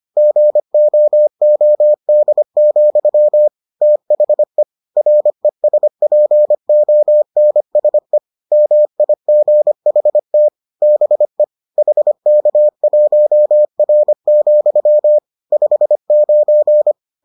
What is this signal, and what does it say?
Morse code, heard as a keyed tone, about 25 wpm, 600 Hz. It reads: GOOD, THE RESPONSE MIGHT BE HK1R, 59